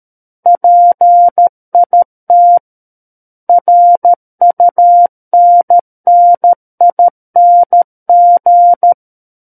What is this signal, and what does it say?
Morse code, heard as a keyed tone, about 13 wpm, 700 Hz. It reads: PIT RUNNING
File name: Book One / 04-Chapter 4 - 13 WPM